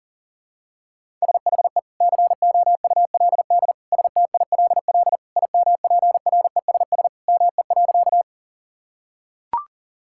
Morse code, heard as a keyed tone, about 40 wpm, 700 Hz. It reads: SHE COULD STILL IMPRESS ME.
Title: Sentences from Top 1600 Words 40wpm